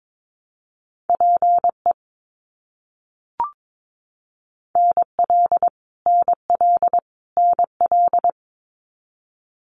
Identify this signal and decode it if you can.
Morse code, heard as a keyed tone, about 22 wpm, 700 Hz. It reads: PE NL NL NL
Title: Canadian Provinces and Territories - Repeated 3x - 22wpm